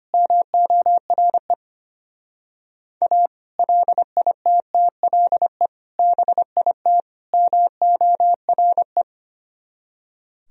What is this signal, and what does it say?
Morse code, heard as a keyed tone, about 31 wpm, 700 Hz. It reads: MORE A LITTLE BIT MORE
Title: Speed-Racing - Sets of 4 Words 25wpm